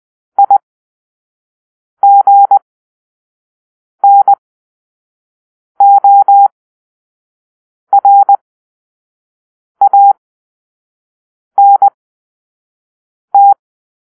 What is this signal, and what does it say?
Morse code, heard as a keyed tone, about 20 wpm, spaced as for 5 wpm, 800 Hz. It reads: IGNORANT
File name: Grp-3 Drill-1